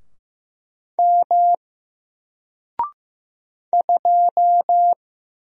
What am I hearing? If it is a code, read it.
Morse code, heard as a keyed tone, about 15 wpm, 700 Hz. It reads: M 2
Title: Single Letter-Number 15wpm